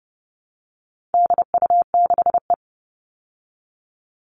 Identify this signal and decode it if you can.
Morse code, heard as a keyed tone, about 30 wpm, 700 Hz. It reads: DU6E
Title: All Cut Numbers - 4-Digit 30wpm